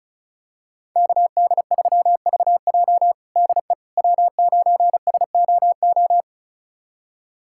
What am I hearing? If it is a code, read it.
Morse code, heard as a keyed tone, about 36 wpm, 700 Hz. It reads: KD3VJ DE W9SOO
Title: USA Callsign DE Callsign - 5 Characters - 35wpm